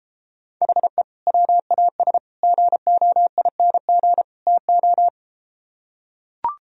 Morse code, heard as a keyed tone, about 32 wpm, 700 Hz. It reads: HE WAS GOING TO